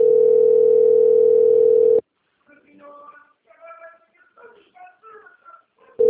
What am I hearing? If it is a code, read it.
Ringback tone (North America), 440 + 480 Hz.